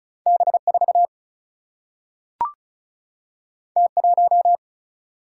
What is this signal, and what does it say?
Morse code, heard as a keyed tone, about 36 wpm, 700 Hz. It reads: B 4 T 1
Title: Single-Letter Single-Number 35wpm